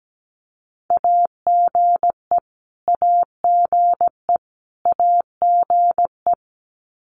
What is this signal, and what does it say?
Morse code, heard as a keyed tone, about 17 wpm, 700 Hz. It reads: AGE AGE AGE
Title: Top 500 Words Repeated 17wpm